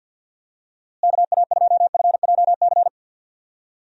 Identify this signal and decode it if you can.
Morse code, heard as a keyed tone, about 50 wpm, 700 Hz. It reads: KA2FJC
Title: USA Call Signs - 6 Characters - Repeated 2x - 50wpm